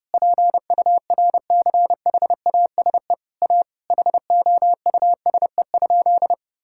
Morse code, heard as a keyed tone, about 30 wpm, 700 Hz. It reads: PURCHASE A HOUSE?
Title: Sentences from Top 1000 Words - Review Part 6 - 30wpm